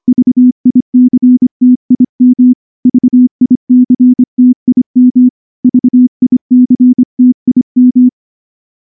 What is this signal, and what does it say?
Morse code, heard as a keyed tone, about 23 wpm, 250 Hz. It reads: VICTIM VICTIM VICTIM
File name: morse-training-25WPM